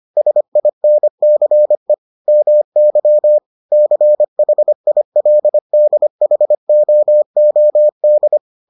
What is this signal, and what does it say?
Morse code, heard as a keyed tone, about 25 wpm, 600 Hz. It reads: SINCE MY CHILDHOOD